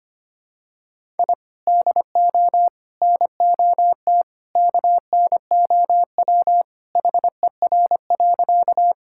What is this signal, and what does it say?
Morse code, heard as a keyed tone, about 25 wpm, 700 Hz. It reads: I DO NOT KNOW HER.
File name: practice / 25wpm / Sentences with 2-Word Phrases - 25wpm